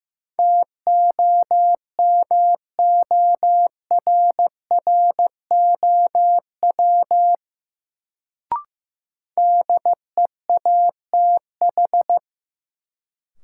Morse code, heard as a keyed tone, about 15 wpm, 700 Hz. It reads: TOMORROW DEATH
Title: Sentences from Top 1100 Words - New Word Drill - 15wpm